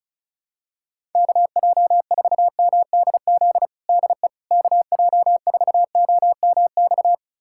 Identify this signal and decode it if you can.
Morse code, heard as a keyed tone, about 36 wpm, 700 Hz. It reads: KJ4MDZ DE KJ4OMX
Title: USA Callsign DE Callsign - 6 Characters - 35wpm